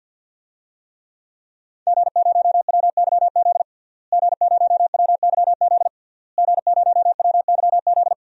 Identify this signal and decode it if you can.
Morse code, heard as a keyed tone, about 46 wpm, 700 Hz. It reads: G0WYZ G0WYZ G0WYZ
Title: DXCC-100 Call Signs - 5 Characters - Repeated 3x - 50wpm